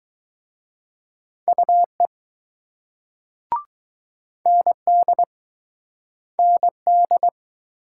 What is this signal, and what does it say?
Morse code, heard as a keyed tone, about 20 wpm, 700 Hz. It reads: UE ND ND